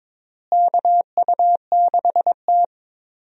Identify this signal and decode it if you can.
Morse code, heard as a keyed tone, about 22 wpm, 700 Hz. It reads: KU6T
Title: USA Call Signs - 4 Characters - Repeated 2x - 22wpm